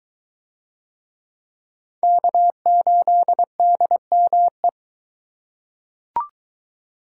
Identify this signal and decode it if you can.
Morse code, heard as a keyed tone, about 22 wpm, 700 Hz. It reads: K8DME